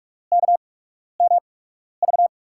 Morse code, heard as a keyed tone, about 48 wpm, spaced as for 12 wpm, 700 Hz. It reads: KMV
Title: USA Call Signs - 5 Characters - 8x Character Spacing - 45wpm